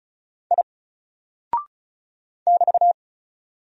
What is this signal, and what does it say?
Morse code, heard as a keyed tone, about 35 wpm, 700 Hz. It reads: I <BT>
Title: Single Character (Letter:Number:Punctuation:Prosign) 35wpm